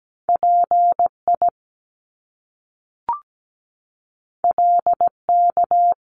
Morse code, heard as a keyed tone, about 17 wpm, 700 Hz. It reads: PI LK
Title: Top Two Letter Combinations in Words 17wpm